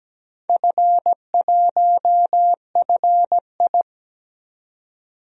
Morse code, heard as a keyed tone, about 17 wpm, 700 Hz. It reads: F1FI
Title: DXCC-100 Call Signs - 4 Characters - 17wpm